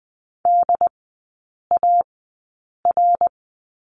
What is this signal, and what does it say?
Morse code, heard as a keyed tone, about 20 wpm, spaced as for 8 wpm, 700 Hz. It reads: DAR
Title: 067 Lesson 12e - Letter d - Three Characters 20wpm